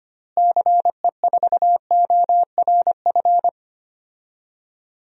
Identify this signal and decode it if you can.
Morse code, heard as a keyed tone, about 25 wpm, 700 Hz. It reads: CE4ORF
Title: DXCC-100 Call Signs - 6 Characters - Repeated 3x - 25wpm